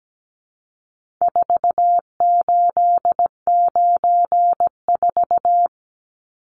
Morse code, heard as a keyed tone, about 17 wpm, 700 Hz. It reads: 4894
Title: Numbers - 4-Digit 17wpm